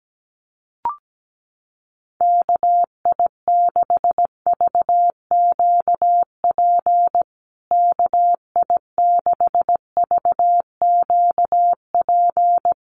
Morse code, heard as a keyed tone, about 17 wpm, 700 Hz. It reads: KI6VQP KI6VQP
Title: USA Call Signs Repeated 3x 17wpm